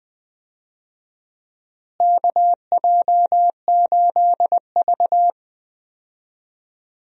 Morse code, heard as a keyed tone, about 21 wpm, 700 Hz. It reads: KJ8V
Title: USA Call Signs - 4 Characters - Repeated 3x - 20wpm